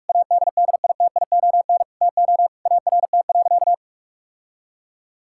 Morse code, heard as a keyed tone, about 41 wpm, 700 Hz. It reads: ADDITION TO ART.
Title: Sentences from Top 1000 Words - Review Part 6 - 45wpm